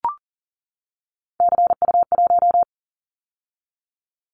Morse code, heard as a keyed tone, about 40 wpm, 700 Hz. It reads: CU1
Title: Callsign Prefixes for Top 100 DXCC - 40wpm